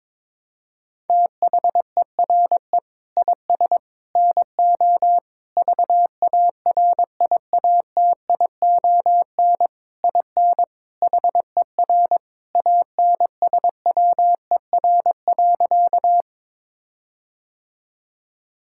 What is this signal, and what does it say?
Morse code, heard as a keyed tone, about 22 wpm, 700 Hz. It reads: THERE IS NO VARIATION IN HER ANSWER.
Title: Sentences from Top 1600 Words 22wpm